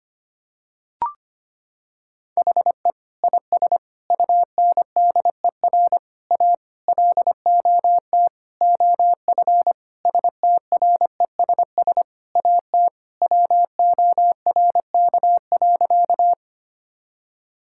Morse code, heard as a keyed tone, about 25 wpm, 700 Hz. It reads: HE IS UNDER A LOT OF STRESS AT WORK.
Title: Sentences from Top 1000 Words - Review Part 6 - 25wpm